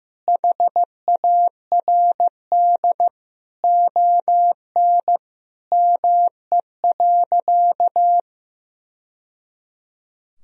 Morse code, heard as a keyed tone, about 16 wpm, 700 Hz. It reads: HARD ON ME.
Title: Sentences from Top 1000 Words - Review Part 1 - 15wpm